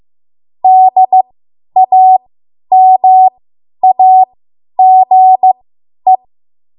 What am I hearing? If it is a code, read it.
Morse code, heard as a keyed tone, about 15 wpm, spaced as for 10 wpm, 750 Hz. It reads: DAMAGE
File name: Archive / 10 WPM / 10WPM